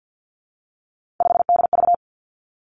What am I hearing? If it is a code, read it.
Morse code, heard as a keyed tone, about 50 wpm, 700 Hz. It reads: 5DV
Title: Three Challenging Letter-Number Combinations 50wpm